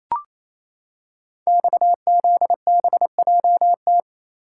Morse code, heard as a keyed tone, about 29 wpm, 700 Hz. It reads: XZBJT
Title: Code Groups - 5 Letter:Numbers - 28wpm